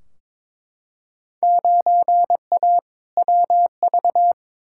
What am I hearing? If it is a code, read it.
Morse code, heard as a keyed tone, about 22 wpm, 700 Hz. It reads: 9A WV
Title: Field Day Exchanges - 22wpm